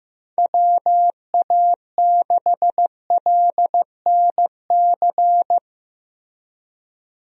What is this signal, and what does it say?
Morse code, heard as a keyed tone, about 15 wpm, 700 Hz. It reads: WA6LNC
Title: USA Call Signs - 6 Characters - Repeated 3x - 15wpm